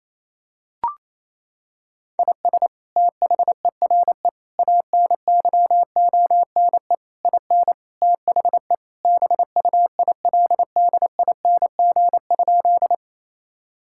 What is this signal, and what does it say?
Morse code, heard as a keyed tone, about 28 wpm, 700 Hz. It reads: IS THERE ANYONE IN THE BUILDING?
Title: Sentences from Top 1000 Words - Review Part 5 - 28wpm